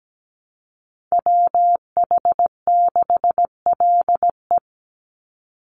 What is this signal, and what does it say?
Morse code, heard as a keyed tone, about 17 wpm, 700 Hz. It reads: WH6LE
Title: NA SOTA Chaser Callsigns 17wpm